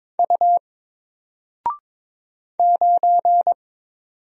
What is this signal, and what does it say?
Morse code, heard as a keyed tone, about 22 wpm, 700 Hz. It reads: U 9
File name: Single-Letter-Number-22wpm